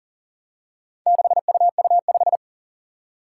Morse code, heard as a keyed tone, about 38 wpm, 700 Hz. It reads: 6UU5
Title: Four Challenging Letter-Number Combinations 40wpm